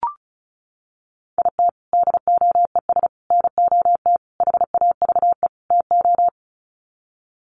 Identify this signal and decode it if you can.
Morse code, heard as a keyed tone, about 34 wpm, 700 Hz. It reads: IT DOES NOT HAVE TO
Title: Sets of 5 Words 35wpm